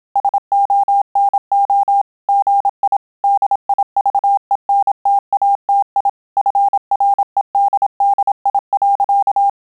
Morse code, heard as a keyed tone, about 27 wpm, 800 Hz. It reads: SONO GI DIVENTATI FREDDI.